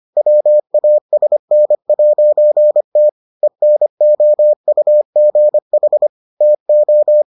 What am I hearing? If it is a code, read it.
Morse code, heard as a keyed tone, about 25 wpm, 600 Hz. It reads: WASN'T ENOUGH TO